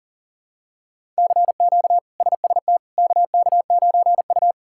Morse code, heard as a keyed tone, about 37 wpm, 700 Hz. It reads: CQ SST KK9U
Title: K1USN SST QSO Practice - 40wpm